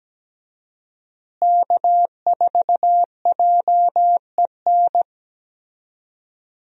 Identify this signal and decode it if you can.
Morse code, heard as a keyed tone, about 17 wpm, 700 Hz. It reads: K4JEN
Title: USA Call Signs - 5 Characters - Repeated 3x - 17wpm